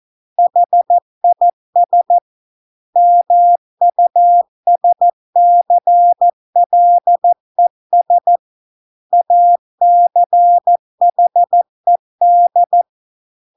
Morse code, heard as a keyed tone, about 14 wpm, 700 Hz. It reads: HIS MUSCLES ACHED